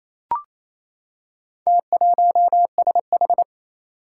Morse code, heard as a keyed tone, about 29 wpm, 700 Hz. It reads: T1SH
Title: Four Challenging Letter-Number Combinations 28wpm